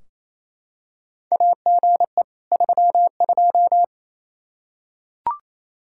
Morse code, heard as a keyed tone, about 28 wpm, 700 Hz. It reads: AGE 32
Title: Practice QSO - Exchange 3 - More Info - 28wpm